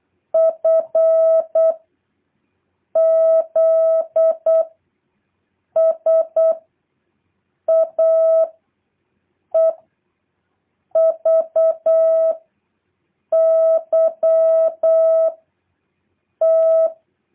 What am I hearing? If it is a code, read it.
Morse code, heard as a keyed tone, about 8 wpm, 650 Hz. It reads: F Z S A E V Y T
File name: jh8cbh / 2023cw